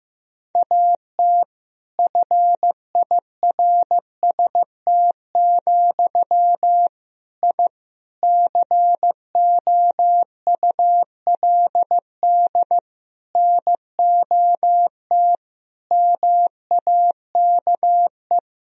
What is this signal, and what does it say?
Morse code, heard as a keyed tone, about 15 wpm, 700 Hz. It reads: AT FIRST, I COULD NOT MAKE
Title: Sentences from Top 1000 Words - Review Part 4 - 15wpm